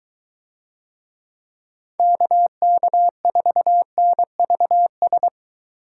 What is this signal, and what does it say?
Morse code, heard as a keyed tone, about 21 wpm, 700 Hz. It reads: KK4NVS